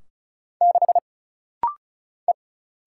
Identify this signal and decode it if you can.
Morse code, heard as a keyed tone, about 35 wpm, 700 Hz. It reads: 6 E
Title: Single Letter-Number - Rapid-Fire 35wpm